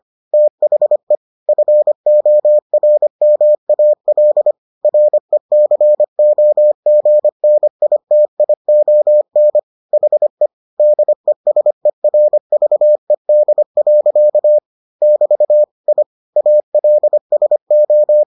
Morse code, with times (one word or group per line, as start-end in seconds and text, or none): 0.3-1.1 THE
1.5-4.5 FORMAL
4.8-9.6 RECOGNITION
9.9-10.5 HE
10.8-14.6 DESERVED.
15.0-15.6 <BT>
15.9-16.0 I
16.4-18.2 ALSO